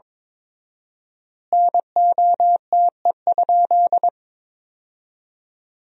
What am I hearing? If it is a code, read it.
Morse code, heard as a keyed tone, about 22 wpm, 700 Hz. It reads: NOTE?
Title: 4x Word-Spacing - Sentences from Top 500 Words - 22wpm